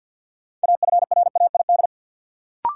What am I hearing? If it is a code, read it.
Morse code, heard as a keyed tone, about 53 wpm, 700 Hz. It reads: AFRAID
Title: Sentences from Top 1500 Words - New Word Drill Alt - 50wpm